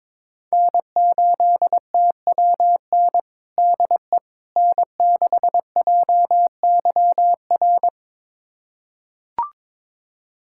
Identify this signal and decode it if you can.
Morse code, heard as a keyed tone, about 22 wpm, 700 Hz. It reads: N8TWN DE N6JYR
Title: USA Callsign DE Callsign 22wpm